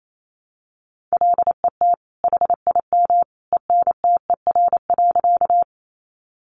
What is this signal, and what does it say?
Morse code, heard as a keyed tone, about 28 wpm, 700 Hz. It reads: LET HIM ENTER.
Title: Sentences from Top 1300 Words 28wpm